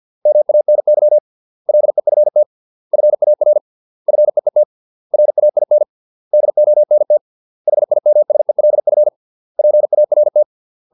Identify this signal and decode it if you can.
Morse code, heard as a keyed tone, about 50 wpm, 600 Hz. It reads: MANY LEFT FAR FEET RAIN DONT HIMSELF PART